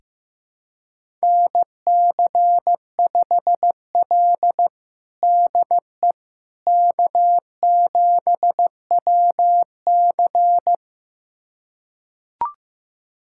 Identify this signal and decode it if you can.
Morse code, heard as a keyed tone, about 16 wpm, 700 Hz. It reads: NC5L DE K7WC
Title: USA Callsign DE Callsign - 4 Characters - 15wpm